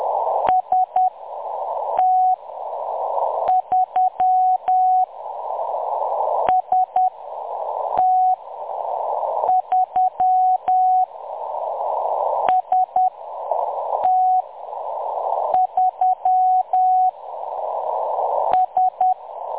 Morse code, heard as a keyed tone, about 10 wpm, spaced as for 6 wpm, 740 Hz. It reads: ST3 ST3 ST3 S
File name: M23-ST3- -0858utc-10184khz